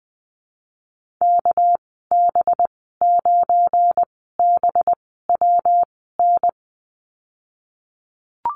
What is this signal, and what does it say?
Morse code, heard as a keyed tone, about 20 wpm, spaced as for 15 wpm, 700 Hz. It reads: KB9BWN
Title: 207 Lesson 33g - Letter j - Call Signs 20wpm